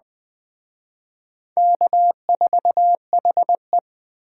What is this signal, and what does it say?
Morse code, heard as a keyed tone, about 20 wpm, 700 Hz. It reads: K4HE
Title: Speed Racing - USA 1x2 Call Signs 17wpm